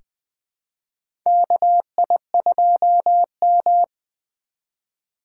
Morse code, heard as a keyed tone, about 20 wpm, 700 Hz. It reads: KI2M
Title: USA Call Signs - 4 Characters - 4x Character Spacing - 20wpm